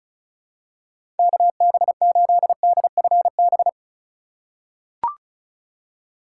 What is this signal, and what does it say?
Morse code, heard as a keyed tone, about 36 wpm, 700 Hz. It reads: KB8DFB
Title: USA Call Signs - 6 Characters - 4x Character Spacing - 35wpm